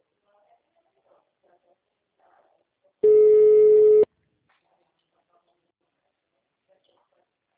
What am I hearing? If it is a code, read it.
Ringback tone (Europe, 425 Hz), with speech.